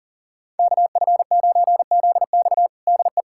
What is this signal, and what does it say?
Morse code, heard as a keyed tone, about 40 wpm, 700 Hz. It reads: KF9ZX DE
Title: USA Callsign DE Callsign - 5 Characters - 40wpm